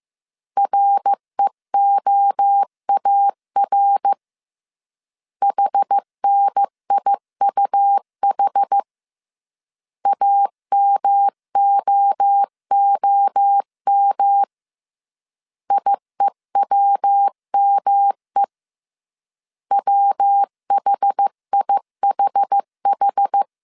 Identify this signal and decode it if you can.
Morse code, heard as a keyed tone, about 15 wpm, 800 Hz. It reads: REOAR HNIUH AMOOM IEWME WHIHH
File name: CW / DET / 1 3 1500C S A 14wpm 2447